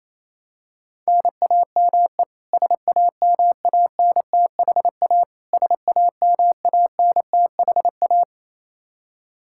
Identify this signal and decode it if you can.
Morse code, heard as a keyed tone, about 28 wpm, 700 Hz. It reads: NAME SAMANTHA SAMANTHA
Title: Practice QSO - Exchange 1 - Beginning - 28wpm